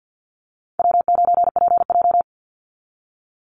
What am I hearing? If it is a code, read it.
Morse code, heard as a keyed tone, about 50 wpm, 700 Hz. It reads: W9PJ
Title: Easy Call Signs Repeated 2x - 50wpm